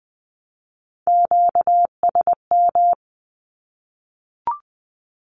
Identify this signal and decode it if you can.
Morse code, heard as a keyed tone, about 20 wpm, 700 Hz. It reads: QSM
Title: Q-Codes Repeated 3x 20wpm